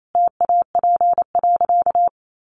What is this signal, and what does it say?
Morse code, heard as a keyed tone, about 28 wpm, 700 Hz. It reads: TAP.
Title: Sentences from Top 1400 Words 28wpm